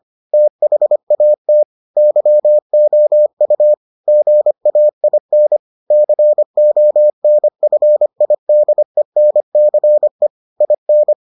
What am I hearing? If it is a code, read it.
Morse code, heard as a keyed tone, about 25 wpm, 600 Hz. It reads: THAT YOU GAIN CONFIDENCE IN